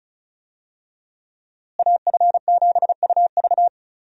Morse code, heard as a keyed tone, about 34 wpm, 700 Hz. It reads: AF7UV
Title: USA Call Signs - 5 Characters - Repeated 2x - 35wpm